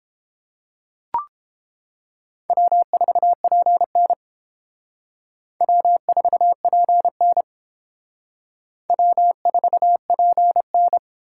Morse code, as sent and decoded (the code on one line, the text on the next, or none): .-- ....- .--. -. / .-- ....- .--. -. / .-- ....- .--. -.
W4PN W4PN W4PN